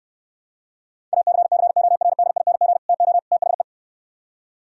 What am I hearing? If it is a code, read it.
Morse code, heard as a keyed tone, about 66 wpm, 700 Hz. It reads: ACCORDING TO THE